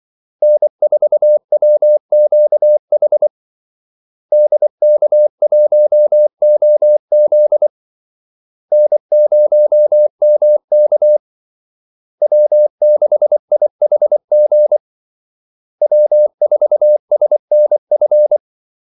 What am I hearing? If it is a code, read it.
Morse code, heard as a keyed tone, about 24 wpm, 600 Hz. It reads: N4WQH DK1OZ N0MK W6IHG W4SNF
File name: Callsigns-24wpm